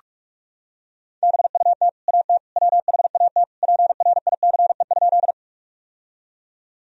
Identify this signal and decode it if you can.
Morse code, heard as a keyed tone, about 43 wpm, 700 Hz. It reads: BUT AT WHAT PRICE?
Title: Sentences from Top 1000 Words - Review Part 4 - 45wpm